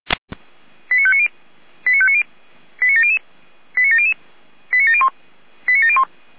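Five-tone selective call (ZVEI), heard as quick runs of five tones, tones of about 80 ms. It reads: 89590 89590 89700 89700 89701 89701
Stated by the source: ZVEI five-tone selective call received by SDR.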